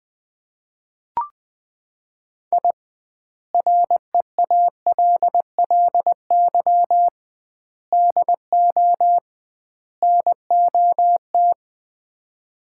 Morse code, heard as a keyed tone, about 20 wpm, 700 Hz. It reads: I REALLY DO NOT
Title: 2x Word-Spacing - Sets of 4 Words - 20wpm